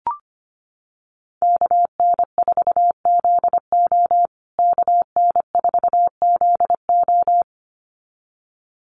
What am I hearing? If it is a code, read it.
Morse code, heard as a keyed tone, about 25 wpm, 700 Hz. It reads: KN4ZO KN4ZO
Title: USA Call Signs Repeated 2x 25wpm